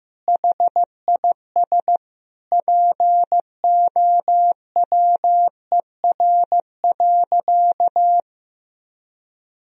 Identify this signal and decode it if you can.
Morse code, heard as a keyed tone, about 16 wpm, 700 Hz. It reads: HIS POWER.
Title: Sentences from Top 1000 Words - Review Part 2 - 15wpm